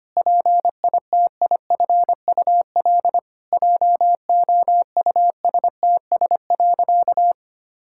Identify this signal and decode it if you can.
Morse code, heard as a keyed tone, about 25 wpm, 700 Hz. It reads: PITIFUL JOUSTS.
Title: Pangram Challenge v2 - 25wpm